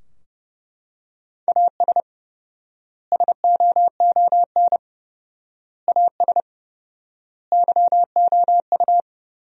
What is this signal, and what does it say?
Morse code, heard as a keyed tone, about 32 wpm, 700 Hz. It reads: AS SOON AS YOU
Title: 4x Word-Spacing - Sentences from Top 400 Words - 30wpm